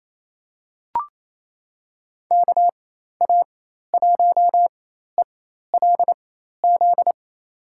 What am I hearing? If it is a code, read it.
Morse code, heard as a keyed tone, about 28 wpm, spaced as for 13 wpm, 700 Hz. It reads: KA1ELZ
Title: USA Call Signs - 6 Characters - 4x Character Spacing - 28wpm